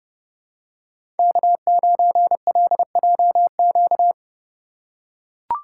Morse code, heard as a keyed tone, about 32 wpm, 700 Hz. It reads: K9LJQ
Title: USA Call Signs - 5 Characters - 4x Character Spacing - 30wpm